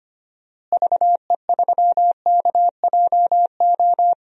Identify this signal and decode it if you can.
Morse code, heard as a keyed tone, about 25 wpm, 700 Hz. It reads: VE3KJO
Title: DXCC-100 Call Signs - 6 Characters - Repeated 2x - 25wpm